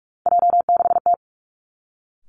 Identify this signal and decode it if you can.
Morse code, heard as a keyed tone, about 46 wpm, 700 Hz. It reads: J6T
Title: Three Challenging Letter-Number Combinations 45wpm